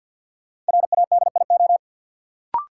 Morse code, heard as a keyed tone, about 52 wpm, 700 Hz. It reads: RADIO
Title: Sentences from Top 1000 Words - Review Part 3 - New Word Drill - 50wpm